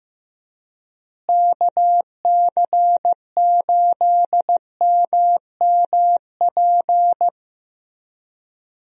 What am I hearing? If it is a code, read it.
Morse code, heard as a keyed tone, about 16 wpm, 700 Hz. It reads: KC8MMP
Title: USA Call Signs - 6 Characters - Repeated 3x - 15wpm